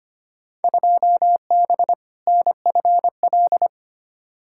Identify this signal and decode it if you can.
Morse code, heard as a keyed tone, about 25 wpm, 700 Hz. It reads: 2B NFL
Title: Field Day Exchanges Encore - 25wpm